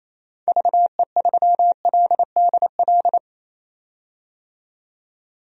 Morse code, heard as a keyed tone, about 28 wpm, 700 Hz. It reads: VE3LDL
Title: DXCC-100 Call Signs - 6 Characters - 28wpm